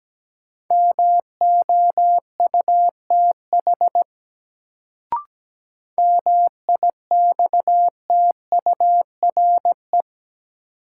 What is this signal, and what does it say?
Morse code, heard as a keyed tone, about 17 wpm, 700 Hz. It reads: MOUTH MIXTURE
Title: Sentences from Top 1500 Words - New Word Drill - 17wpm